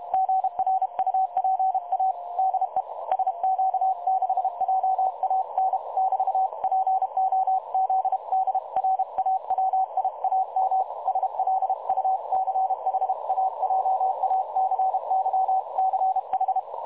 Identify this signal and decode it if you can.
Morse code, heard as a keyed tone, about 33 wpm, 760 Hz. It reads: GRUPA DE SQ6JAN <BT> POZDRAWIAM SLUCHAJACYCH
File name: dlazaawansowanych / nagraniazpasma / websdr recording 3565.0kHz